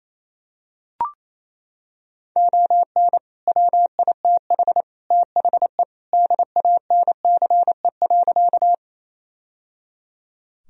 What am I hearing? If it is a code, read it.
Morse code, heard as a keyed tone, about 28 wpm, 700 Hz. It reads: ON WITH THE DANCE.
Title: Sentences from Top 1000 Words - Review Part 4 - 28wpm